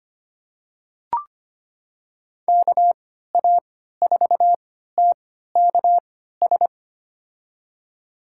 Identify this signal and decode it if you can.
Morse code, heard as a keyed tone, about 25 wpm, spaced as for 14 wpm, 700 Hz. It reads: KA4TKS
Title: USA Call Signs - 6 Characters - 3x Character Spacing - 25wpm